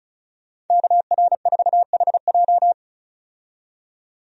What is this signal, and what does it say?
Morse code, heard as a keyed tone, about 34 wpm, 700 Hz. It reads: KR4HJ
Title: USA Call Signs - 5 Characters - 35wpm